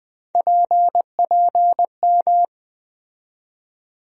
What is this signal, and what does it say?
Morse code, heard as a keyed tone, about 20 wpm, 700 Hz. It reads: PPM
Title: USA Call Sign Postfixes - 3 Letters 20wpm